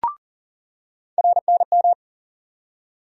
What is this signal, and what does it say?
Morse code, heard as a keyed tone, about 40 wpm, 700 Hz. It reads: RNM
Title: USA Call Sign Postfixes - 3 Letters 40wpm